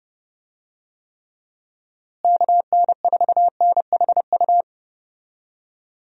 Morse code, heard as a keyed tone, about 30 wpm, 700 Hz. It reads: KN4NHU
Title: USA Call Signs - 6 Characters - Repeated 2x - 30wpm